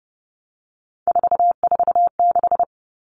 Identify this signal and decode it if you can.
Morse code, heard as a keyed tone, about 30 wpm, 700 Hz. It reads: 446
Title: Numbers - 3-Digit 30wpm